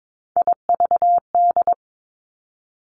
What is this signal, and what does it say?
Morse code, heard as a keyed tone, about 22 wpm, 700 Hz. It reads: IVD
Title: Three Challenging Letter-Number Combinations 22wpm